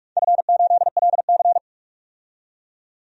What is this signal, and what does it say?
Morse code, heard as a keyed tone, about 43 wpm, 700 Hz. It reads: F8LC
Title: DXCC-100 Call Signs - 4 Characters - 45wpm